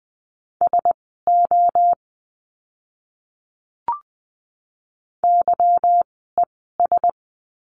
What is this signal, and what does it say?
Morse code, heard as a keyed tone, about 20 wpm, spaced as for 14 wpm, 700 Hz. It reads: SO YES
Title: 146 Lesson 24f - Letter y - Words 20wpm